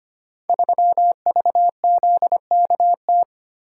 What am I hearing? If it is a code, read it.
Morse code, heard as a keyed tone, about 25 wpm, 700 Hz. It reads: 3VZKT
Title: Code Groups - 5 Letter:Numbers - 25wpm